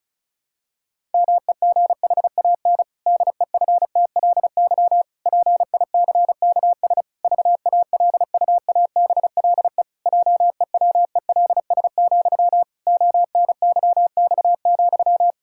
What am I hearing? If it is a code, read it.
Morse code, heard as a keyed tone, about 36 wpm, 700 Hz. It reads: MEGHAN DEFTLY PICKS VALUABLE JEWELS, ONYX,
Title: Pangram Challenge v2 - 35wpm